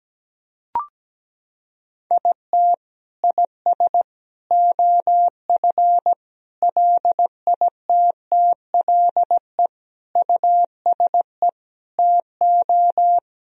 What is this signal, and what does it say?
Morse code, heard as a keyed tone, about 17 wpm, 700 Hz. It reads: IT IS OF LITTLE USE TO
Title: Sentences from Top 1000 Words - Review Part 1 - 17wpm